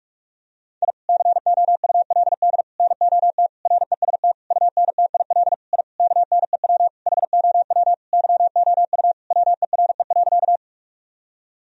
Morse code, heard as a keyed tone, about 41 wpm, 700 Hz. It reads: I COULD NOT REST UNTIL I KNEW HOW YOU WERE.
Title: Sentences from Top 1000 Words - Review Part 1 - 45wpm